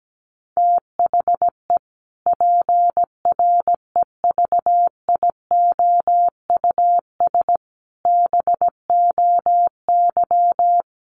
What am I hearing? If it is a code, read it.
Morse code, heard as a keyed tone, about 17 wpm, 700 Hz. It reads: THE PREVIOUS BOY
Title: Sentences from Top 1500 Words 17wpm